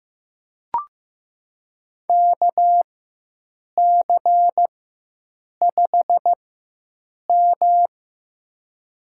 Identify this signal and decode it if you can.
Morse code, heard as a keyed tone, about 15 wpm, spaced as for 7 wpm, 700 Hz. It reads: KC5M
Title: USA Call Signs - 4x Character Spacing - 15wpm